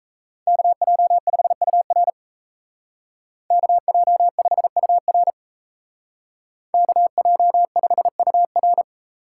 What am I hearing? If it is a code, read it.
Morse code, heard as a keyed tone, about 36 wpm, 700 Hz. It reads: KJ5UR KJ5UR KJ5UR